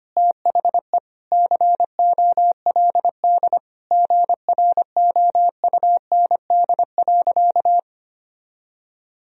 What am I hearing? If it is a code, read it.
Morse code, heard as a keyed tone, about 25 wpm, 700 Hz. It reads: THE COLD GROUND.
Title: Sentences from Top 1000 Words - Review Part 3 - 25wpm